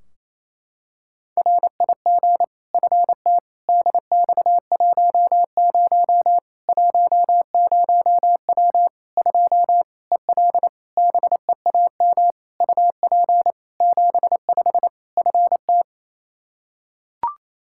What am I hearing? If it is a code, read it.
Morse code, heard as a keyed tone, about 28 wpm, 700 Hz. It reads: RIG FT DX10 10W 2 EL BEAM UP 75 FT
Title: K1USN SST QSO Practice 28wpm